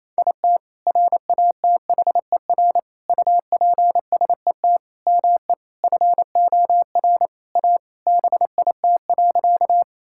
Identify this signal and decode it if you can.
Morse code, heard as a keyed tone, about 28 wpm, 700 Hz. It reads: IT RATHER UPSET ME FOR A BIT.
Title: Sentences from Top 1000 Words - Review Part 5 - 28wpm